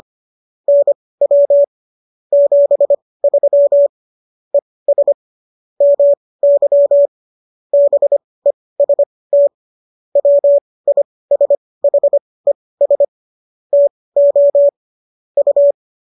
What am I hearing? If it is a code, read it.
Morse code, heard as a keyed tone, about 25 wpm, spaced as for 18 wpm, 570 Hz. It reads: NW 73 ES MY BEST WISHES TO U
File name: qso218 18